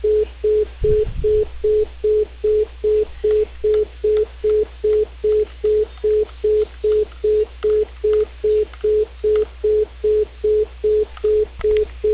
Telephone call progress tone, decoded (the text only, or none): congestion tone